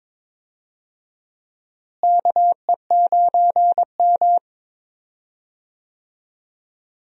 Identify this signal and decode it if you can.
Morse code, heard as a keyed tone, about 22 wpm, 700 Hz. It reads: KE9M
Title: USA Call Signs - 4 Characters - Repeated 3x - 22wpm